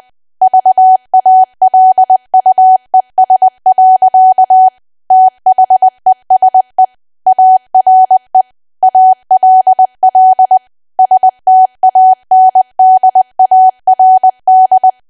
Morse code, heard as a keyed tone, about 20 wpm, 750 Hz. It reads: VALUES. THESE ARE ALL STANDARD
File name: cwa / W1AW-Level3 / 20WPM